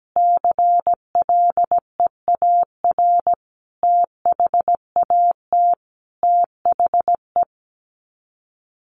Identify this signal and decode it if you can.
Morse code, heard as a keyed tone, about 17 wpm, 690 Hz. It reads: CLEAR THAT THE
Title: Sets of 5 Words Encore 17wpm